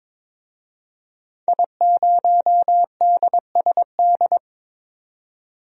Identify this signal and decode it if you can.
Morse code, heard as a keyed tone, about 22 wpm, 700 Hz. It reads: I0DSD
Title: DXCC-100 Call Signs - 5 Characters - 22wpm